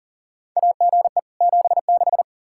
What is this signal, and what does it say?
Morse code, heard as a keyed tone, about 40 wpm, 700 Hz. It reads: AGE 76
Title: Practice QSO - Exchange 3 - More Info - 40wpm